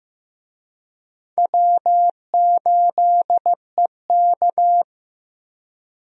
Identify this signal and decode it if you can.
Morse code, heard as a keyed tone, about 15 wpm, 700 Hz. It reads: W8EK
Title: USA Call Signs - 4 Characters - Repeated 2x - 15wpm